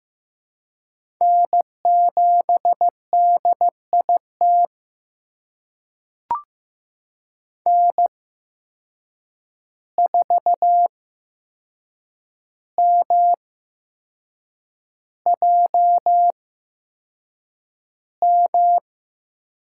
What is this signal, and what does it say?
Morse code, heard as a keyed tone, about 15 wpm, spaced as for 6 wpm, 700 Hz. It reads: N7DIT N4MJM
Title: USA Call Signs - 8x Character Spacing - 15wpm